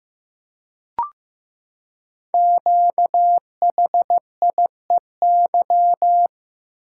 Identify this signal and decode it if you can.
Morse code, heard as a keyed tone, about 15 wpm, 700 Hz. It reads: QHIEY
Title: Code Groups - 5 Letters - 15wpm